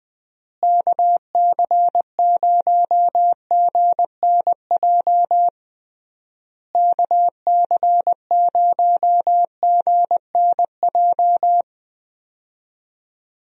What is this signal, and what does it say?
Morse code, heard as a keyed tone, about 20 wpm, 700 Hz. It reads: KC0GNJ KC0GNJ
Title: USA Call Signs - 6 Characters - Repeated 3x - 20wpm